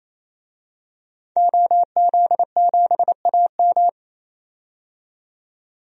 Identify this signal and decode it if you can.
Morse code, heard as a keyed tone, about 28 wpm, 700 Hz. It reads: OZ7AM
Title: DXCC-100 Call Signs - 5 Characters - 28wpm